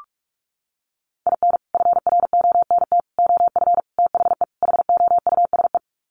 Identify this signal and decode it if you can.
Morse code, heard as a keyed tone, about 44 wpm, 700 Hz. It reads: IN FRONT OF THE HOUSE
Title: Sets of 5 Words Encore2 45wpm